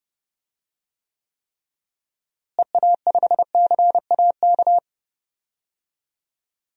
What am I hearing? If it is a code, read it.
Morse code, heard as a keyed tone, about 32 wpm, 700 Hz. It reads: EA5CAK
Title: DXCC-100 Call Signs - 6 Characters - Repeated 3x - 30wpm